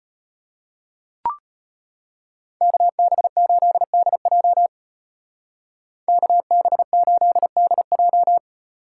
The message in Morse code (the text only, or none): KB8DJ KB8DJ